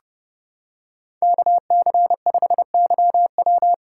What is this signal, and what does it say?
Morse code, heard as a keyed tone, about 30 wpm, 700 Hz. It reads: KC5YW